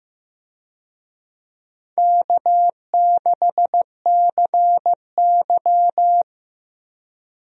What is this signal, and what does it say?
Morse code, heard as a keyed tone, about 15 wpm, 700 Hz. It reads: K6CY
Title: USA Call Signs - 4 Characters - 15wpm